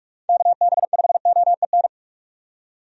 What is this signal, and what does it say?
Morse code, heard as a keyed tone, about 41 wpm, 700 Hz. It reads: KB5OEN
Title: USA Call Signs - 6 Characters - 45wpm